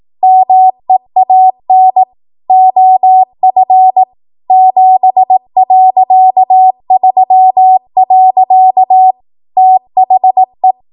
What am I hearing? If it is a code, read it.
Morse code, heard as a keyed tone, about 18 wpm, 750 Hz. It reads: MEAN OF 7.3. THE
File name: Morse / 18WPM